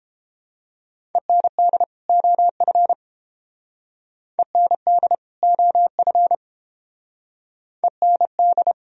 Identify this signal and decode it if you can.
Morse code, heard as a keyed tone, about 31 wpm, 700 Hz. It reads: END OF END OF END